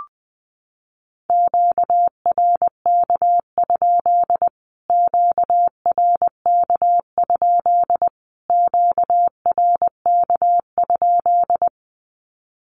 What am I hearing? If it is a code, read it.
Morse code, heard as a keyed tone, about 20 wpm, 700 Hz. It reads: QRK? QRK? QRK?
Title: Q-Codes Repeated 3x 20wpm